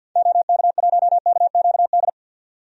Morse code, heard as a keyed tone, about 51 wpm, 700 Hz. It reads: OK1KQD
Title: DXCC-100 Call Signs - 6 Characters - 50wpm